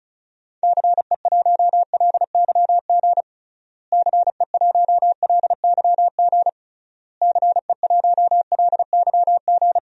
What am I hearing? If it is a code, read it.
Morse code, heard as a keyed tone, about 34 wpm, 700 Hz. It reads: CE1LYG CE1LYG CE1LYG
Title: DXCC-100 Call Signs - 6 Characters - Repeated 3x - 35wpm